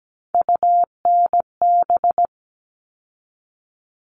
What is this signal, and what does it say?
Morse code, heard as a keyed tone, about 17 wpm, 700 Hz. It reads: UNB
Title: All Cut Numbers - 3-Digit 17wpm